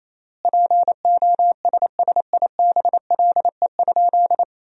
Morse code, heard as a keyed tone, about 28 wpm, 700 Hz. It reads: POSSIBLE?
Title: 3x Word-Spacing - Sentences from Top 500 Words - 28wpm